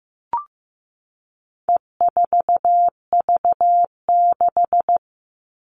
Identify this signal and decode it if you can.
Morse code, heard as a keyed tone, about 15 wpm, 700 Hz. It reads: E4V6
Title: All Cut Numbers - 4-Digit 15wpm